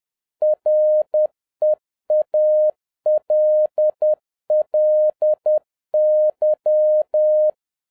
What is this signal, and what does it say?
Morse code, heard as a keyed tone, about 10 wpm, 610 Hz. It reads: REALLY
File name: 2011-07-mp3 / n56 BACK FROM A BIG ACCIDENT 10wpm